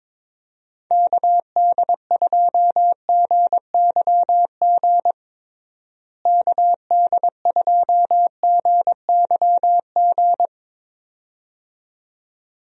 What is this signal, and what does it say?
Morse code, heard as a keyed tone, about 22 wpm, 700 Hz. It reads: KD2GYG KD2GYG
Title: USA Call Signs - 6 Characters - Repeated 2x - 22wpm